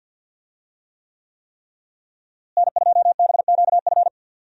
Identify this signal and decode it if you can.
Morse code, heard as a keyed tone, about 52 wpm, 700 Hz. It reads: N2BQF
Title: USA Call Signs - 5 Characters - Repeated 3x - 50wpm